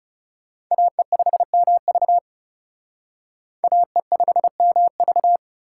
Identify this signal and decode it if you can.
Morse code, heard as a keyed tone, about 37 wpm, 700 Hz. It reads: AE5MV AE5MV
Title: Speed Racing - USA 2x2 Call Signs 30wpm